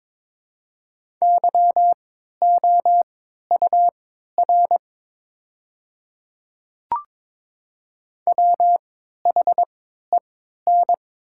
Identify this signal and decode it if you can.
Morse code, heard as a keyed tone, about 22 wpm, spaced as for 12 wpm, 700 Hz. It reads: YOUR WHEN
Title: 3x Character Spacing - Top 100 Words - 22wpm